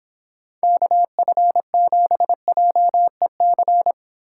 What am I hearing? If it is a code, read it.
Morse code, heard as a keyed tone, about 25 wpm, 700 Hz. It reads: KF7JEC